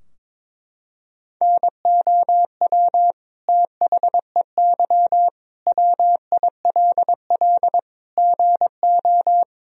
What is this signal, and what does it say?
Morse code, heard as a keyed tone, about 21 wpm, 700 Hz. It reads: NOW THEY WILL GO
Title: Sentences from Top 1000 Words - Review Part 4 - 22wpm